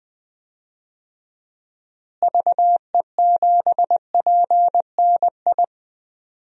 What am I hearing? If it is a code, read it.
Morse code, heard as a keyed tone, about 20 wpm, 700 Hz. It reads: VE7PNI
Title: DXCC-100 Call Signs - 6 Characters - Repeated 3x - 20wpm